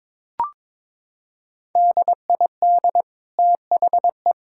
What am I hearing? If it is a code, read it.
Morse code, heard as a keyed tone, about 22 wpm, 700 Hz. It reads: DID THE
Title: Sentences from Top 1000 Words - Review Part 5 - 22wpm